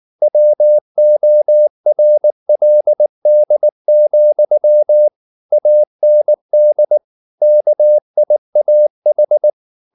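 Morse code, heard as a keyed tone, about 19 wpm, 600 Hz. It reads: WORLD, AND KIAH
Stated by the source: Morse code at 19 wpm from CW med Gnister, reading Ken Follett's novel Never in English.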